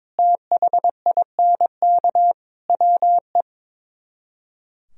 Morse code, heard as a keyed tone, about 24 wpm, 700 Hz. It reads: THINK WE
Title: Speed-Racing - Sets of 4 Words 22wpm